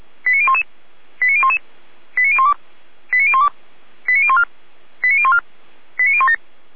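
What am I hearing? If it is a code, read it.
Five-tone selective call (ZVEI), heard as quick runs of five tones, tones of about 70 ms. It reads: 89011 89011 89012 89012 89014 89014 89017
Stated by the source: Five-tone ZVEI selective call 89010, received by SDR, from fire brigade Orlamünde (feuerwehr-orlamuende).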